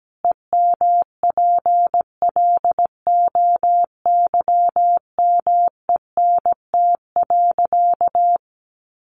Morse code, heard as a keyed tone, about 17 wpm, 700 Hz. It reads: EMPLOYMENT.
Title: Sentences from Top 1500 Words 17wpm